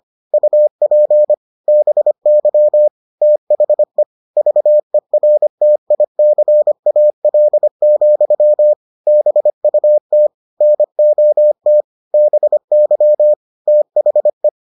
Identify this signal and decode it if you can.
Morse code, heard as a keyed tone, about 25 wpm, 600 Hz. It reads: UP BY THE VERTICAL, BUT NOT BY THE